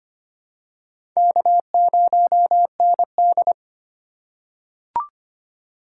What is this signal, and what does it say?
Morse code, heard as a keyed tone, about 25 wpm, 700 Hz. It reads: K0ND
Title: USA Call Signs - 4x Character Spacing - 25wpm